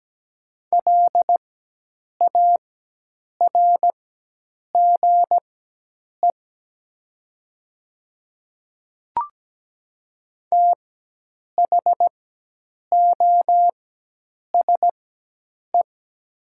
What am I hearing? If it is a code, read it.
Morse code, heard as a keyed tone, about 17 wpm, spaced as for 8 wpm, 700 Hz. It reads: LARGE THOSE
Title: 4x Character Spacing - Top 300 Words - 17wpm